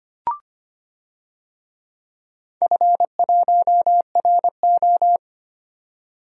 Morse code, heard as a keyed tone, about 25 wpm, 700 Hz. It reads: F1RO
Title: DXCC-100 Call Signs - 4 Characters - 25wpm